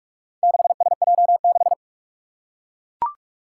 Morse code, heard as a keyed tone, about 42 wpm, 700 Hz. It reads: 6SJ6
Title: Four Challenging Letter-Number Combinations 45wpm